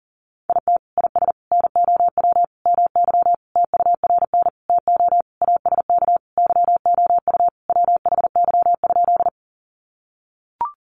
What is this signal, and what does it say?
Morse code, heard as a keyed tone, about 40 wpm, 700 Hz. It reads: IT IS NOW MY TURN TO ASK YOU WHY?
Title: Sentences from Top 200 Words 40wpm